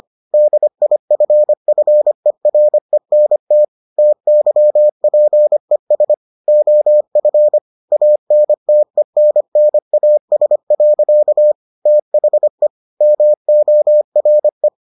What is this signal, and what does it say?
Morse code, heard as a keyed tone, about 25 wpm, 600 Hz. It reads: DIFFERENT TYPES OF ANTENNAS. THE MORE